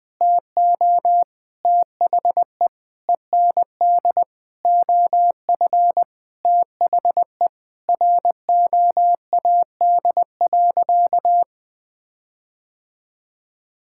Morse code, heard as a keyed tone, about 20 wpm, 700 Hz. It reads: TO THE END OF THE ROAD.
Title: Sentences with 5-Word Phrases - 20wpm